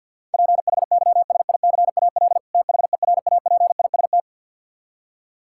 Morse code, heard as a keyed tone, about 51 wpm, 700 Hz. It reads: PHYSICAL THERAPIST
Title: Practice QSO - Exchange 3 - More Info - 50wpm